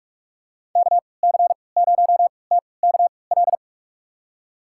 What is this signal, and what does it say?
Morse code, heard as a keyed tone, about 46 wpm, spaced as for 26 wpm, 690 Hz. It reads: KC0TKL
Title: USA Call Signs - 6 Characters - 3x Character Spacing - 45wpm